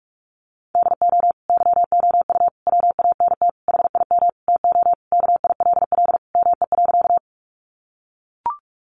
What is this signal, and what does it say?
Morse code, heard as a keyed tone, about 44 wpm, 700 Hz. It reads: DO YOU WANT HIM TO KILL ME.
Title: Sentences from Top 1400 Words 45wpm